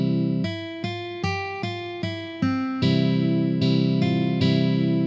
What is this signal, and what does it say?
There is no code.